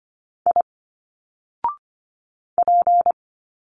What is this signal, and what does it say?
Morse code, heard as a keyed tone, about 25 wpm, 700 Hz. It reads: I P
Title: Single Letters 25wpm